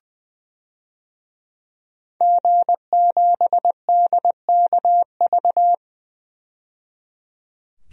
Morse code, heard as a keyed tone, about 20 wpm, 700 Hz. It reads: G7DKV
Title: DXCC-100 Call Signs - 5 Characters - 20wpm